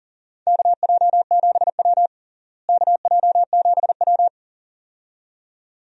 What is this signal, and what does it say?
Morse code, heard as a keyed tone, about 40 wpm, 700 Hz. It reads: KJ7W KJ7W
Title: USA Call Signs - 4 Characters - Repeated 2x - 40wpm